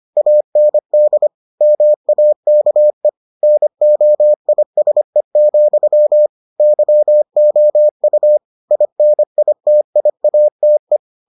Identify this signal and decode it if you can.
Morse code, heard as a keyed tone, about 25 wpm, 600 Hz. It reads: AND MAKE NOISE, YOU INITIATE